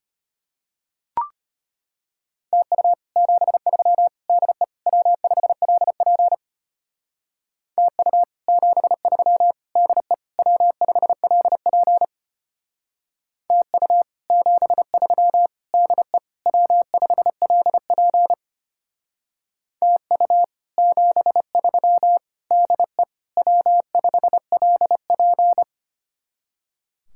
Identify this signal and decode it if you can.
Morse code, heard as a keyed tone, about 32 wpm, 700 Hz. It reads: TU 73 DE W5LP TU 73 DE W5LP TU 73 DE W5LP TU 73 DE W5LP